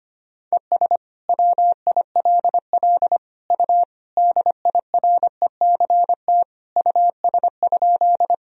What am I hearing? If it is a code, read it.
Morse code, heard as a keyed tone, about 25 wpm, 700 Hz. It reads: ES WILL U DIRECT US?
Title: Sentences from Top 400 Words and Abbreviations 25wpm